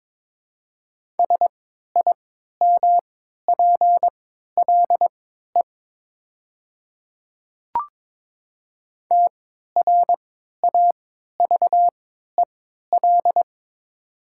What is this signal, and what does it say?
Morse code, heard as a keyed tone, about 22 wpm, spaced as for 13 wpm, 690 Hz. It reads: SIMPLE TRAVEL
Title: 3x Character Spacing - Top 500 Words - 22wpm